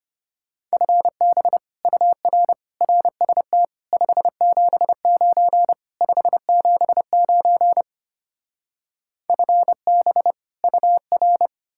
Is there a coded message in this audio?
Yes: Morse code from 0.7 to 11.5 s, heard as a keyed tone, about 33 wpm, 700 Hz. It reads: FB UR RST 579 579 FB UR